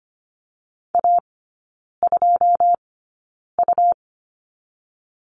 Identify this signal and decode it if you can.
Morse code, heard as a keyed tone, about 25 wpm, spaced as for 9 wpm, 700 Hz. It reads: A2U
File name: Lesson 17e - Letter u - Three Characters 25wpm